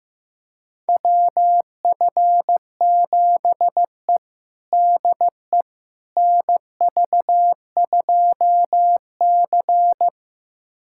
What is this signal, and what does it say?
Morse code, heard as a keyed tone, about 15 wpm, 700 Hz. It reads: WF7E DE NV2C
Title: USA Callsign DE Callsign - 4 Characters - 15wpm